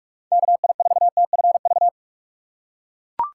Morse code, heard as a keyed tone, about 42 wpm, 700 Hz. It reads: KI4TFV
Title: USA Call Signs - 6 Characters - 2x Character Spacing - 45wpm